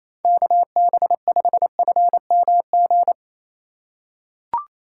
Morse code, heard as a keyed tone, about 27 wpm, 700 Hz. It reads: KB5FMG
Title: USA Call Signs - 6 Characters - 3x Character Spacing - 28wpm